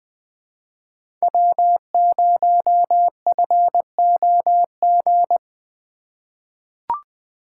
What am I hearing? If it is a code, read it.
Morse code, heard as a keyed tone, about 20 wpm, 700 Hz. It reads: W0FOG
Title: USA Call Signs - 5 Characters - 3x Character Spacing - 20wpm